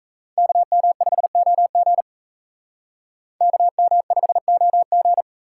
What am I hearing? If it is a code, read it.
Morse code, heard as a keyed tone, about 34 wpm, 700 Hz. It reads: KM5OG KM5OG
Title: Speed Racing - USA 2x2 Call Signs 28wpm